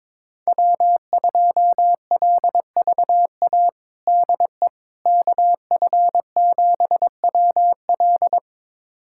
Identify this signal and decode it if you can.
Morse code, heard as a keyed tone, about 22 wpm, 700 Hz. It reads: W2LVA DE KF7WL
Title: USA Callsign DE Callsign - 5 Characters - 22wpm